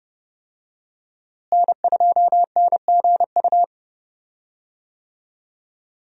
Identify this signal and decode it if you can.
Morse code, heard as a keyed tone, about 29 wpm, 700 Hz. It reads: N2NGU
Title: USA Call Signs - 5 Characters - Repeated 3x - 30wpm